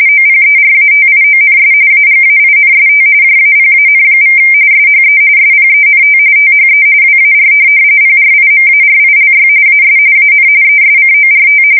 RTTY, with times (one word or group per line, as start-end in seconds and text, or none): 0.1-1.7 OPERATIONS
1.9-3.0 AGAINST
3.2-4.4 JAPAN=
4.7-5.2 SHE
5.4-6.2 SPENT
6.4-6.8 THE
7.0-7.5 WAR
8.0-9.5 PRIMARILY
9.7-10.0 AS
10.2-10.5 AN
10.6-11.6 ESCORT